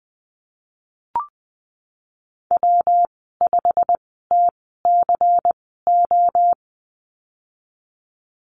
Morse code, heard as a keyed tone, about 20 wpm, spaced as for 14 wpm, 700 Hz. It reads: W5TCO
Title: 167 Lesson 27g - Number 7 - Call Signs 20wpm